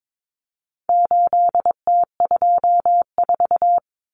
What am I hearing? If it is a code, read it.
Morse code, heard as a keyed tone, about 22 wpm, 700 Hz. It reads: 8T24
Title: Common Cut Numbers - 4-Digit 22wpm